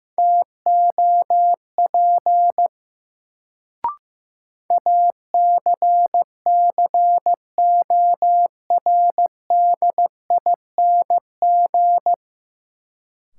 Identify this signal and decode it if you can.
Morse code, heard as a keyed tone, about 16 wpm, 700 Hz. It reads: TOP ACCORDING
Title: Sentences from Top 700 Words - New Word Drill - 15wpm